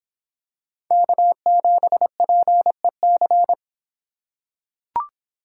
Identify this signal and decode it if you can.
Morse code, heard as a keyed tone, about 25 wpm, 700 Hz. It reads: K7PEC